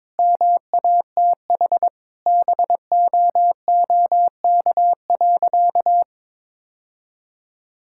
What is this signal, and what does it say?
Morse code, heard as a keyed tone, about 21 wpm, 700 Hz. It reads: MATH BOOK.
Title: Sentences from Top 1000 Words - Review Part 6 - 22wpm